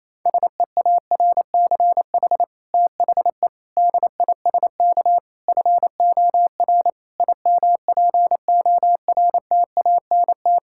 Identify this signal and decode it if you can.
Morse code, heard as a keyed tone, about 28 wpm, 700 Hz. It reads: SEARCH THE DISK FOR IMPORTANT
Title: Sentences from Top 1000 Words - Review Part 6 - 28wpm